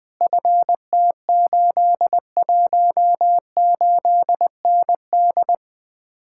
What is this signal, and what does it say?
Morse code, heard as a keyed tone, about 21 wpm, 700 Hz. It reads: FT818ND
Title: Speed-Racing Common QSO Phrases - 20wpm